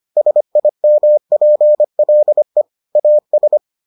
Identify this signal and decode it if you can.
Morse code, heard as a keyed tone, about 25 wpm, 600 Hz. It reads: SIMPLE AS